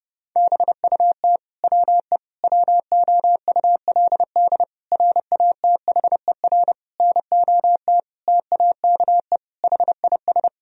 Morse code, heard as a keyed tone, about 30 wpm, 700 Hz. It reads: BUT WE WOULD RATHER NOT TAKE HIS
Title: Sentences from Top 1000 Words - Review Part 4 - 30wpm